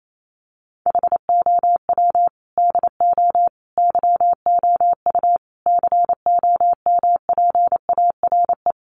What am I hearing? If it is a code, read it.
Morse code, heard as a keyed tone, about 28 wpm, 700 Hz. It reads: HOW DO YOU COMPARE
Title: Sentences from Top 1300 Words 28wpm